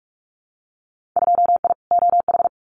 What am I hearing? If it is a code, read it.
Morse code, heard as a keyed tone, about 46 wpm, 700 Hz. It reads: 2I OH
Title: Winter Field Day Exchanges 45wpm